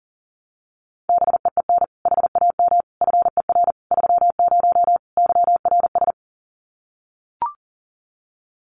Morse code, heard as a keyed tone, about 40 wpm, 700 Hz. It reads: BEEN HAM FER 30 YRS
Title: Common QSO Phrases Encore 40wpm